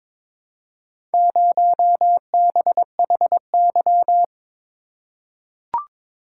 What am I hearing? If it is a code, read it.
Morse code, heard as a keyed tone, about 22 wpm, 700 Hz. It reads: 0BHY
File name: practice / 22wpm / Four Letter-Number Combinations - 22wpm